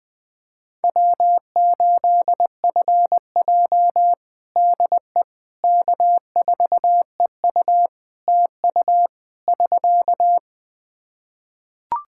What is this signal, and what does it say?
Morse code, heard as a keyed tone, about 21 wpm, 700 Hz. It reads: W8FJ DE K4EU TU <SK>
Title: Practice QSO - Exchange 4 - Ending - 20wpm